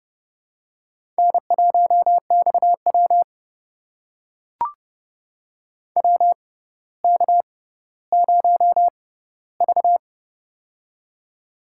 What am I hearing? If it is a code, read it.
Morse code, heard as a keyed tone, about 30 wpm, spaced as for 15 wpm, 700 Hz. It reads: N1XW WK0V
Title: USA Call Signs - 4 Characters - 6x Character Spacing - 30wpm